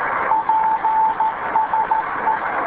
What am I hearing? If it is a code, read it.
Morse code, heard as a keyed tone, about 13 wpm, 880 Hz, with radio noise. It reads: PSE